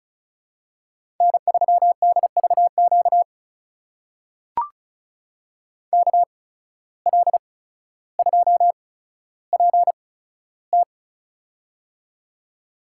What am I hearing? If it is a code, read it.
Morse code, heard as a keyed tone, about 36 wpm, spaced as for 15 wpm, 700 Hz. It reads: N3DVQ KL2PT
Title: USA Call Signs - 5 Characters - 8x Character Spacing - 35wpm